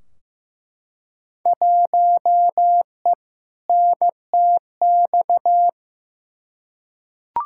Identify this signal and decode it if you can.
Morse code, heard as a keyed tone, about 15 wpm, 700 Hz. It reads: 1E NTX
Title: Field Day Exchanges - 15wpm